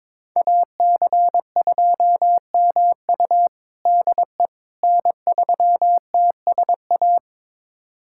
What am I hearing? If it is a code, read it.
Morse code, heard as a keyed tone, about 22 wpm, 700 Hz. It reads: AC2MU DE N3TSA
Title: USA Callsign DE Callsign - 5 Characters - 22wpm